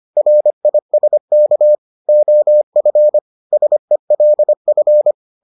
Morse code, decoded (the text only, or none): RISK OF SELF